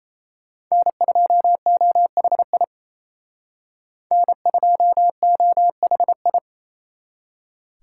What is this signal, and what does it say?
Morse code, heard as a keyed tone, about 36 wpm, 700 Hz. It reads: N2OHI N2OHI